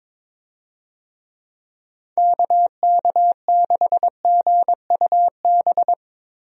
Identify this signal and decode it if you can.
Morse code, heard as a keyed tone, about 22 wpm, 700 Hz. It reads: KK6GUB
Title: USA Call Signs - 6 Characters - Repeated 3x - 22wpm